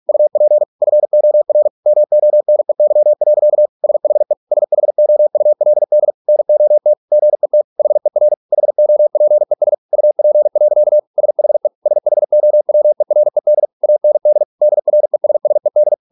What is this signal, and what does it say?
Morse code, heard as a keyed tone, about 46 wpm, 600 Hz. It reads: UP FOR MONEY. SHE SHOULD NOT GET HER HOPES UP. SHE SHOWERED AND DRESSED